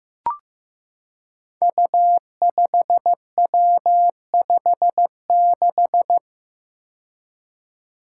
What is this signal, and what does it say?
Morse code, heard as a keyed tone, about 15 wpm, 700 Hz. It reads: U5W56
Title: Five Challenging Letter-Number Combinations 15wpm